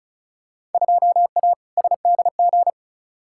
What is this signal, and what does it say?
Morse code, heard as a keyed tone, about 34 wpm, 700 Hz. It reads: 2A SDG
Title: Field Day Exchanges Encore - 35wpm